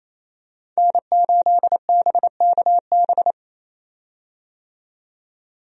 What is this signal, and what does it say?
Morse code, heard as a keyed tone, about 28 wpm, 700 Hz. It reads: N8BKB
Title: USA Call Signs - 5 Characters - 28wpm